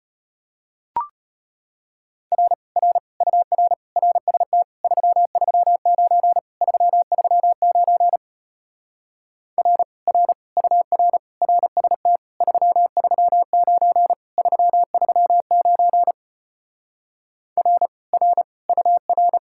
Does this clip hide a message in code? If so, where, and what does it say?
Yes: Morse code from 2.3 to 19.4 s, heard as a keyed tone, about 33 wpm, 700 Hz. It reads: R R UR RST 339 339 R R UR RST 339 339 R R UR